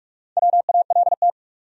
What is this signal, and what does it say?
Morse code, heard as a keyed tone, about 42 wpm, 700 Hz. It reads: WALT
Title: USA Common Names - Max 4 Letters - 45wpm